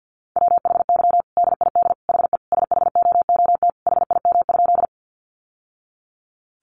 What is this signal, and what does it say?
Morse code, heard as a keyed tone, about 50 wpm, 700 Hz. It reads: WHY DID HE SHOOT HIM?
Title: Sentences from Top 1200 Words 50wpm